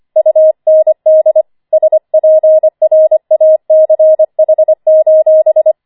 Morse code, heard as a keyed tone, about 25 wpm, 630 Hz. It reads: UND SPRACH:
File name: 25wpm